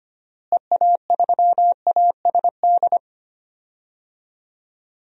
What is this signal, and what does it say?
Morse code, heard as a keyed tone, about 25 wpm, 700 Hz. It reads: EA3ASD
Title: DXCC-100 Call Signs - 6 Characters - Repeated 3x - 25wpm